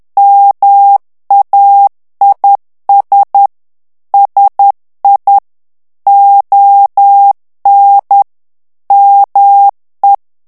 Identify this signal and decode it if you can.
Morse code, heard as a keyed tone, about 10 wpm, 800 Hz. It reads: MAIS SI ON ME
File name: FRANCAIS TEXTE CLAIR 2 VITESSE 10 MOTS MINUTES